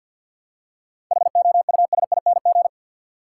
Morse code, heard as a keyed tone, about 50 wpm, 700 Hz. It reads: HOUSING
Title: Seven Letter Words - 50wpm